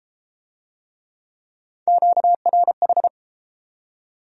Morse code, heard as a keyed tone, about 28 wpm, 700 Hz. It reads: QRH